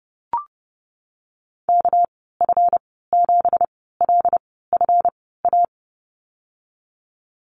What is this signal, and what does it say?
Morse code, heard as a keyed tone, about 32 wpm, spaced as for 17 wpm, 700 Hz. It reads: KF7LFA
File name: Lesson 37g - Letter x - Call Signs 30wpm